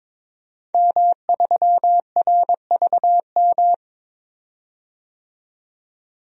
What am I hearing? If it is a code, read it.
Morse code, heard as a keyed tone, about 22 wpm, 700 Hz. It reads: M3RVM
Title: DXCC-100 Call Signs - 5 Characters - 22wpm